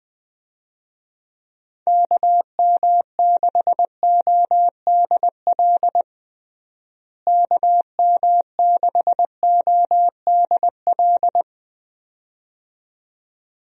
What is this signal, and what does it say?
Morse code, heard as a keyed tone, about 20 wpm, 700 Hz. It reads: KM6ODL KM6ODL
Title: USA Call Signs - 6 Characters - Repeated 2x - 20wpm